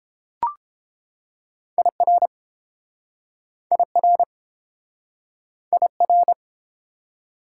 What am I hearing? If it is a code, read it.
Morse code, heard as a keyed tone, about 31 wpm, 700 Hz. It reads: IR IR IR